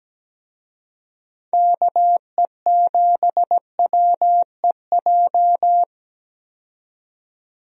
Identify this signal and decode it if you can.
Morse code, heard as a keyed tone, about 17 wpm, 700 Hz. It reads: KE7WEJ
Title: USA Call Signs - 6 Characters - 17wpm